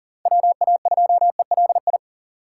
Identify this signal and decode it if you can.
Morse code, heard as a keyed tone, about 52 wpm, 700 Hz. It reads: WA2ELI